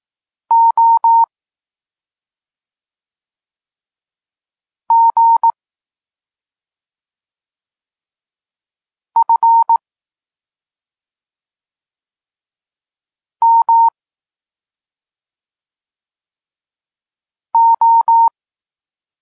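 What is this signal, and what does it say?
Morse code, heard as a keyed tone, about 18 wpm, 920 Hz. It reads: O G F M O
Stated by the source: Polish Morse lesson teaching the letter C, from SP9MOA Niepołomicki Klub Krótkofalowców.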